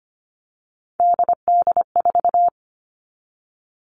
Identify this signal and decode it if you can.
Morse code, heard as a keyed tone, about 25 wpm, 700 Hz. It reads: DD4
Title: All Cut Numbers - 3-Digit 25wpm